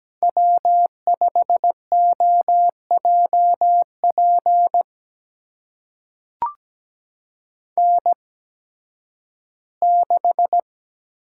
Morse code, heard as a keyed tone, about 17 wpm, spaced as for 11 wpm, 700 Hz. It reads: W5OJP N6
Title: USA Call Signs - 5 Characters - 8x Character Spacing - 17wpm